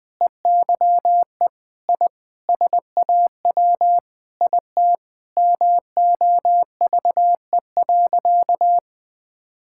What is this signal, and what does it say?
Morse code, heard as a keyed tone, about 21 wpm, 700 Hz. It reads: EYE I SAW IT MOVE.
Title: Sentences with 8-Word Phrases - 20wpm